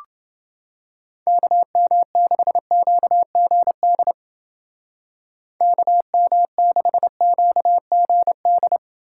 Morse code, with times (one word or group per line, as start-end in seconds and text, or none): 1.3-4.1 KM6QGD
5.6-8.8 KM6QGD